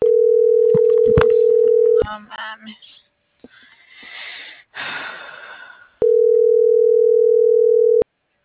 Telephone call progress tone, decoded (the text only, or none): ringback tone